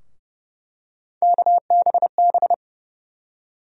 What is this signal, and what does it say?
Morse code, heard as a keyed tone, about 30 wpm, 700 Hz. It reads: KBB
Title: USA Call Sign Postfixes - 3 Letters 30wpm